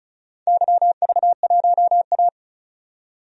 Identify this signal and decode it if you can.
Morse code, heard as a keyed tone, about 34 wpm, 700 Hz. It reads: YV1A
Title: DXCC-100 Call Signs - 4 Characters - 35wpm